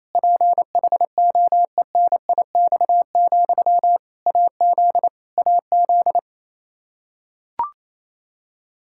Morse code, heard as a keyed tone, about 28 wpm, 700 Hz. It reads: PHOENIX, AZ AZ
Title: K1USN SST QSO Practice 28wpm